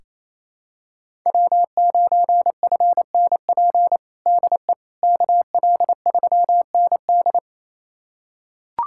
Morse code, heard as a keyed tone, about 28 wpm, 700 Hz. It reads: W9FNP DE KL3ND
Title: USA Callsign DE Callsign - 5 Characters - 28wpm